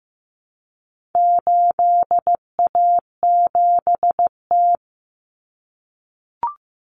Morse code, heard as a keyed tone, about 15 wpm, 700 Hz. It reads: 8A7T
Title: Common Cut Numbers - 4-Digit 15wpm